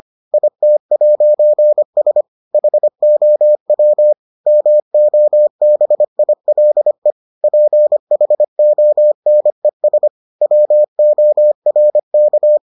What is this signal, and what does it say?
Morse code, heard as a keyed tone, about 25 wpm, 600 Hz. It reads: IT'S HOW MOBILE PHONES WORK